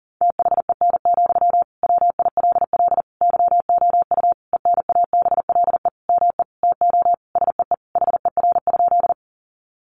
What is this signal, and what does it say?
Morse code, heard as a keyed tone, about 40 wpm, 700 Hz. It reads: THEN, WILL YOU ENABLE ME TO SEE HER?
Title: Sentences from Top 1500 Words 40wpm